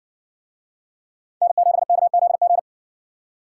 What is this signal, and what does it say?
Morse code, heard as a keyed tone, about 58 wpm, 700 Hz. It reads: N7KZG